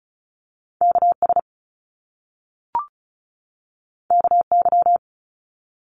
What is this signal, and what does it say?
Morse code, heard as a keyed tone, about 36 wpm, 700 Hz. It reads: KS KY
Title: US States 35wpm